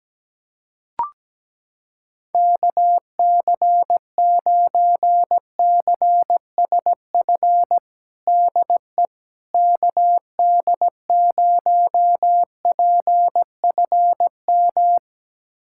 Morse code, heard as a keyed tone, about 17 wpm, 700 Hz. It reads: KC9CSF DE KD0PFM
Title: USA Callsign DE Callsign - 6 Characters - 17wpm